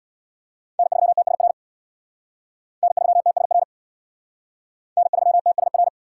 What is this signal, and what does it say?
Morse code, heard as a keyed tone, about 65 wpm, 700 Hz. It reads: N3TSR N3TSR N3TSR